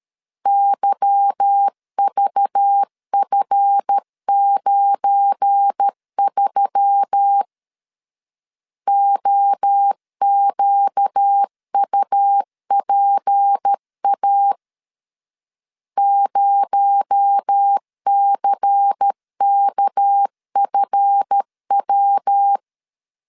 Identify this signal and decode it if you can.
Morse code, heard as a keyed tone, about 13 wpm, 800 Hz. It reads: YVF93 OQUPA 0CKFW
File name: LEZIONI / 8 YJ45 / 8 Y 12wpm 2228